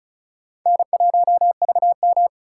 Morse code, heard as a keyed tone, about 34 wpm, 700 Hz. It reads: N1VM
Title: USA Call Signs - 4 Characters - 4x Character Spacing - 35wpm